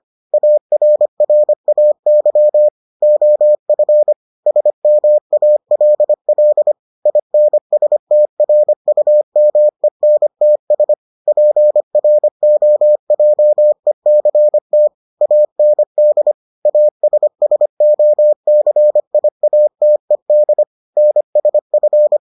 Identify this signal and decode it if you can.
Morse code, heard as a keyed tone, about 25 wpm, 600 Hz. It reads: ARRAY OF SMALL INSTRUMENTS PROJECT AND ASSOCIATED NSF